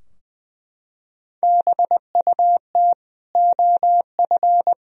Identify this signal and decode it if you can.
Morse code, heard as a keyed tone, about 20 wpm, 700 Hz. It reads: BUT OF
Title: Sentences from Top 1000 Words - Review Part 2 - 20wpm